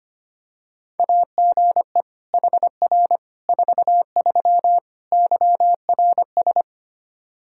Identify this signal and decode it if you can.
Morse code, heard as a keyed tone, about 25 wpm, 700 Hz. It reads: AGE HR 43 YRS
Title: Practice QSO - Exchange 3 - More Info - 25wpm